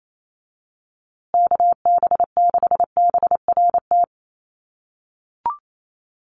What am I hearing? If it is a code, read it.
Morse code, heard as a keyed tone, about 28 wpm, 700 Hz. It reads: KB6BRT
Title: USA Call Signs Repeated 3x 28wpm